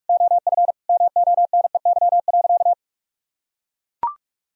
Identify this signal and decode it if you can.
Morse code, heard as a keyed tone, about 41 wpm, 700 Hz. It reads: OF MONEY.
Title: Sentences from Top 1000 Words - Review Part 3 - 45wpm